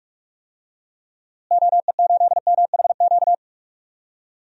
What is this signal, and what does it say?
Morse code, heard as a keyed tone, about 45 wpm, 700 Hz. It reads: OE8MHQ
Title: DXCC-100 Call Signs - 6 Characters - Repeated 2x - 45wpm